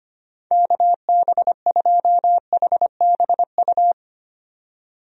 Morse code, heard as a keyed tone, about 25 wpm, 700 Hz. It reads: KB2HBU
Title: USA Call Signs - 6 Characters - 25wpm